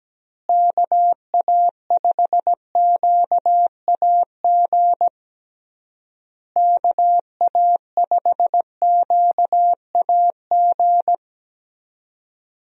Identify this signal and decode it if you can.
Morse code, heard as a keyed tone, about 17 wpm, 700 Hz. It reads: KA5QAG KA5QAG
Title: USA Call Signs - 6 Characters - Repeated 3x - 17wpm